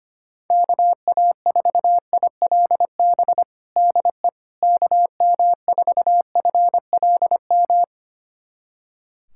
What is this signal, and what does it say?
Morse code, heard as a keyed tone, about 25 wpm, 700 Hz. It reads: KA4ILB DE KM4FLM
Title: USA Callsign DE Callsign 25wpm